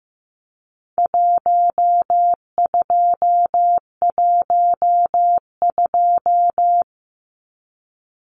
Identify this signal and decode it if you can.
Morse code, heard as a keyed tone, about 15 wpm, 700 Hz. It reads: 1212
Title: Numbers - Contest Style - 4-Digit 15wpm